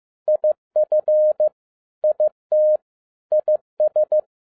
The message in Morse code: IF IT IS

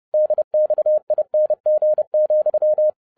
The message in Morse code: DXING,

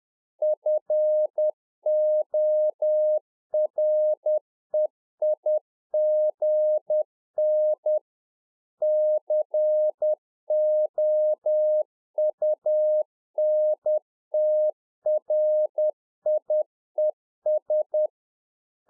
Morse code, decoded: FOREIGN COUNTRIES

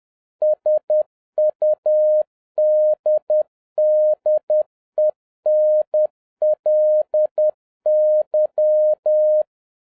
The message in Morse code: SUDDENLY

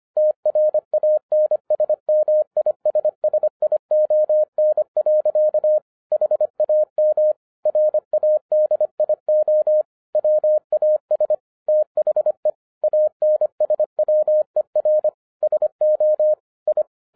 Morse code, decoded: TRANSMISSION. HAM RADIO WAS THE ANSWER SO I

